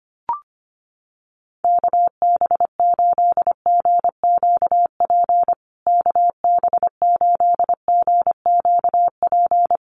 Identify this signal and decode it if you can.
Morse code, heard as a keyed tone, about 25 wpm, 700 Hz. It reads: KB8GQP KB8GQP
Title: USA Call Signs Repeated 3x 25wpm